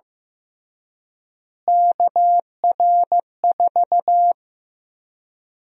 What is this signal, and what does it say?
Morse code, heard as a keyed tone, about 15 wpm, 700 Hz. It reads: KR4
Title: USA Call Sign Prefixes - Repeated 2x - 15wpm